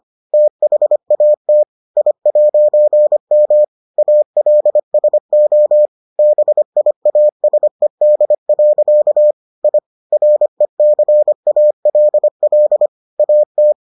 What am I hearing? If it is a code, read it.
Morse code, heard as a keyed tone, about 25 wpm, 600 Hz. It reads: THAT I'M ALSO BIASED. I RECALL AT